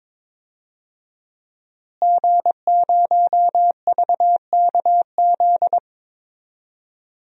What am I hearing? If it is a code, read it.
Morse code, heard as a keyed tone, about 22 wpm, 700 Hz. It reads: G0VKZ
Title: DXCC-100 Call Signs - 5 Characters - Repeated 2x - 22wpm